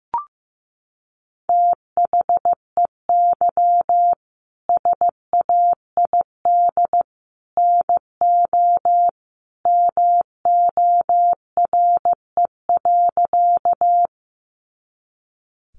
Morse code, heard as a keyed tone, about 15 wpm, 700 Hz. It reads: THEY SAID NO MORE.
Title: Sentences from Top 100 Words 15wpm